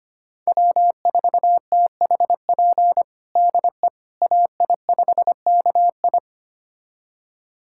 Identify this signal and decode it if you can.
Morse code, heard as a keyed tone, about 25 wpm, 700 Hz. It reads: W4THP DE AI5KI
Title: USA Callsign DE Callsign - 5 Characters - 25wpm